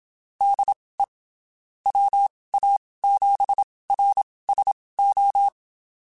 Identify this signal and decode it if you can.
Morse code, heard as a keyed tone, about 27 wpm, spaced as for 19 wpm, 800 Hz. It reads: DE WA7RSO
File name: (spaced) / QSOs Spaced-mp3s / WD0FBT @25 sp